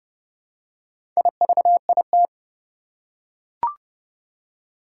Morse code, heard as a keyed tone, about 30 wpm, 700 Hz. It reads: IVIT
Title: Four Challenging Letter-Number Combinations 30wpm